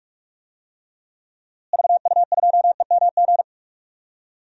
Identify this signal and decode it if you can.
Morse code, heard as a keyed tone, about 46 wpm, 700 Hz. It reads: VU2EMG
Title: DXCC-100 Call Signs - 6 Characters - Repeated 2x - 45wpm